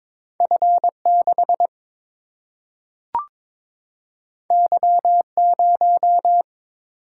Morse code, heard as a keyed tone, about 22 wpm, 700 Hz. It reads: F6 Y0
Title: Single-Letter Single-Number 22wpm